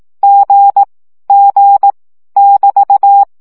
Morse code, heard as a keyed tone, about 18 wpm, 800 Hz. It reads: G G <BT>
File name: CW gyakorlas g 15 baud